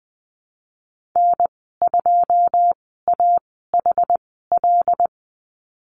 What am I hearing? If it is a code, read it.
Morse code, heard as a keyed tone, about 20 wpm, spaced as for 14 wpm, 700 Hz. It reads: N2AHL
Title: 082 Lesson 14g - Number 2 - Call Signs 20wpm